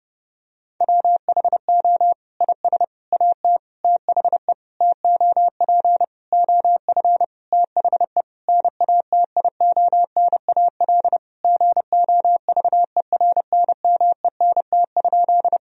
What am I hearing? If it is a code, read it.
Morse code, heard as a keyed tone, about 30 wpm, 700 Hz. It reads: WHO IS AT THE TOP OF THE NATIONAL GOVERNMENT?
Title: Sentences from Top 1000 Words - Review Part 4 - 30wpm